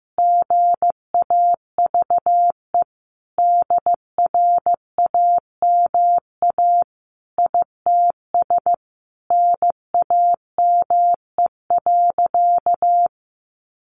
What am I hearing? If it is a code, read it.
Morse code, heard as a keyed tone, about 15 wpm, 700 Hz. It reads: GAVE DRAMA ITS NAME.
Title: Sentences from Top 1500 Words 15wpm